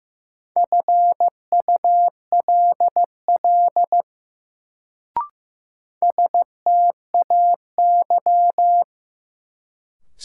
Morse code, heard as a keyed tone, about 15 wpm, 700 Hz. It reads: FULL STAY
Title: Sentences from Top 500 Words - New Word Drill - 15wpm